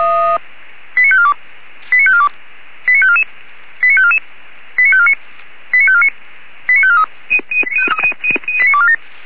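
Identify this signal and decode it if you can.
Five-tone selective call (ZVEI), heard as quick runs of five tones, tones of about 70 ms. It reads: 89542 89542 89544 89544 89549 89549 89543
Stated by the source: ZVEI five-tone selective call received on an SDR.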